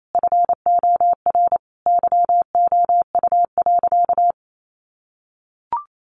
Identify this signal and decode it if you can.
Morse code, heard as a keyed tone, about 28 wpm, 700 Hz. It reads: FOR YOU.
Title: Sentences from Top 1100 Words 28wpm